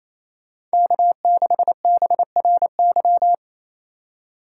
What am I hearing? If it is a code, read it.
Morse code, heard as a keyed tone, about 33 wpm, 700 Hz. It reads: K6BRY
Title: Speed Racing - USA 1x3 Call Signs 28wpm